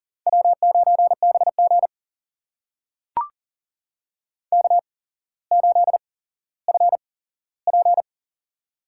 Morse code, heard as a keyed tone, about 42 wpm, spaced as for 17 wpm, 700 Hz. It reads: W9BG K8FP
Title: USA Call Signs - 4 Characters - 8x Character Spacing - 40wpm